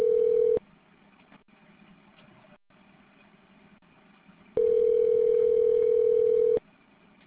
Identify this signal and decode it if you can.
Ringback tone (North America), 440 + 480 Hz.